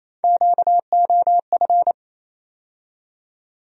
Morse code, heard as a keyed tone, about 28 wpm, 700 Hz. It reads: QOF
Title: USA Call Sign Postfixes - 3 Letters 28wpm